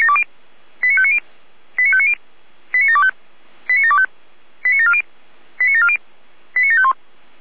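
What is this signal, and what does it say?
Five-tone selective call (ZVEI), heard as quick runs of five tones, tones of about 70 ms. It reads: 89590 89590 89725 89725 89740 89740 89751
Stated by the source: ZVEI five-tone call received on SDR, 455 channel.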